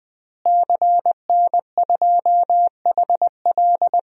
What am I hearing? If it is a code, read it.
Morse code, heard as a keyed tone, about 20 wpm, 700 Hz. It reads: CN2HL
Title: DXCC-100 Call Signs - 5 Characters - Repeated 3x - 20wpm